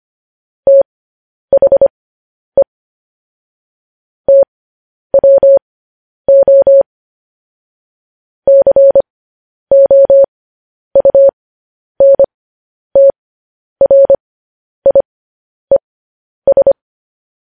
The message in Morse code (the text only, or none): THE TWO COUNTRIES